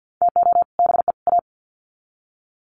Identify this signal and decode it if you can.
Morse code, heard as a keyed tone, about 50 wpm, 700 Hz. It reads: TO BE A
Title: Sets of 3 Words 50wpm